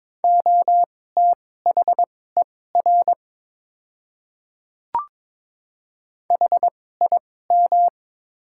Morse code, heard as a keyed tone, about 22 wpm, spaced as for 16 wpm, 700 Hz. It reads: OTHER HIM
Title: 2x Character Spacing - Top 100 Words - 22wpm